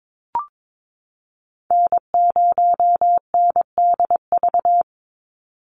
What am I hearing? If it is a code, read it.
Morse code, heard as a keyed tone, about 22 wpm, 700 Hz. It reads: N0NDV
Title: USA Call Signs 22wpm